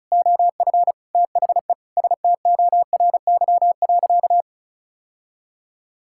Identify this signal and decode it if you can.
Morse code, heard as a keyed tone, about 34 wpm, 700 Hz. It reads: OF THE STORY.
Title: Sentences from Top 1000 Words - Review Part 6 - 35wpm